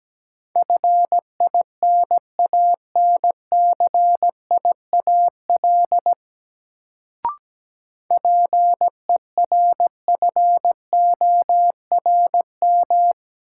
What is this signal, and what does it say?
Morse code, heard as a keyed tone, about 17 wpm, 700 Hz. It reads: FINANCIAL PERFORM
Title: Sentences from Top 1000 Words - Review Part 5 - New Word Drill - 17wpm